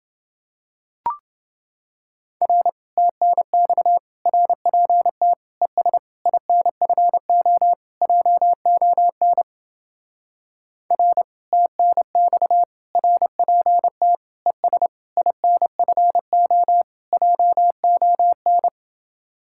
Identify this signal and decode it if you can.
Morse code, heard as a keyed tone, about 28 wpm, 700 Hz. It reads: R TNX RPT ES INFO JON R TNX RPT ES INFO JON